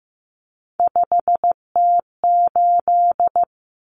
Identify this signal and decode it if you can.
Morse code, heard as a keyed tone, about 15 wpm, 700 Hz. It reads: 5T8
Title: Common Cut Numbers - 3-Digit 15wpm